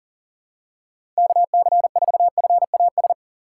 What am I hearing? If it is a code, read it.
Morse code, heard as a keyed tone, about 40 wpm, 700 Hz. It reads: KC4FAS
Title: USA Call Signs - 6 Characters - 3x Character Spacing - 40wpm